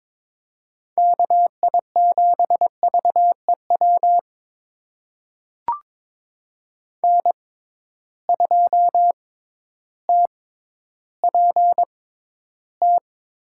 Morse code, heard as a keyed tone, about 21 wpm, spaced as for 12 wpm, 700 Hz. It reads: KI7VEW N2TPT
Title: USA Call Signs - 6x Character Spacing - 22wpm